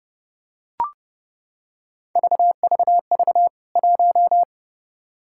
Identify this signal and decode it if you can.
Morse code, heard as a keyed tone, about 30 wpm, 700 Hz. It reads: VVV 1
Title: K1USN SST QSO Practice 30wpm